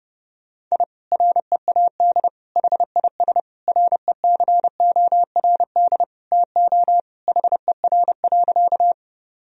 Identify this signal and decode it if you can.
Morse code, heard as a keyed tone, about 30 wpm, 700 Hz. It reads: I READ HIS RECORD TO HER.
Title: Sentences from Top 1000 Words - Review Part 3 - 30wpm